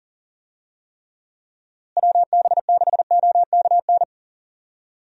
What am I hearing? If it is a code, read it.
Morse code, heard as a keyed tone, about 40 wpm, 700 Hz. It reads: WB6OKN
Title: USA Call Signs - 6 Characters - Repeated 3x - 40wpm